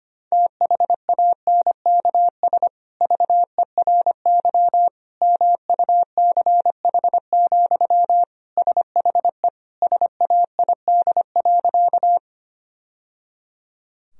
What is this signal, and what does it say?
Morse code, heard as a keyed tone, about 25 wpm, 700 Hz. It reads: THANKS VERY MUCH, SHE SAID.
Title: Sentences from Top 1000 Words - Review Part 3 - 25wpm